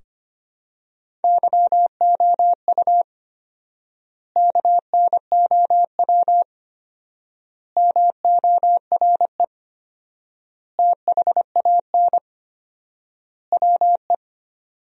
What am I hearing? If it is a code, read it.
Morse code, heard as a keyed tone, about 25 wpm, 700 Hz. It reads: YOU KNOW MORE THAN WE
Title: 4x Word-Spacing - Sentences from Top 100 Words - 25wpm